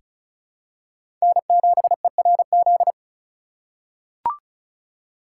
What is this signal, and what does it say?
Morse code, heard as a keyed tone, about 34 wpm, 700 Hz. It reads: N7ERZ
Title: USA Call Signs - 5 Characters - 4x Character Spacing - 35wpm